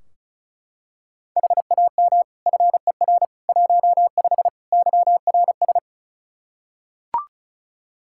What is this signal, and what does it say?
Morse code, heard as a keyed tone, about 36 wpm, 700 Hz. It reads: HAM FER 15 YRS
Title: Practice QSO - Exchange 3 - More Info - 35wpm